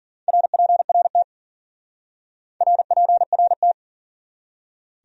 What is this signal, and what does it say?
Morse code, heard as a keyed tone, about 51 wpm, 700 Hz. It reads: RPRT RPRT